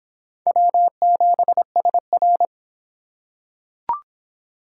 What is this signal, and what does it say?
Morse code, heard as a keyed tone, about 24 wpm, 700 Hz. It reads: W7SR